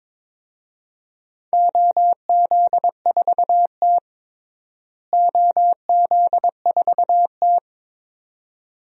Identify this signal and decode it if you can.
Morse code, heard as a keyed tone, about 22 wpm, 700 Hz. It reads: OZ4T OZ4T
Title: DXCC-100 Call Signs - 4 Characters - Repeated 2x - 22wpm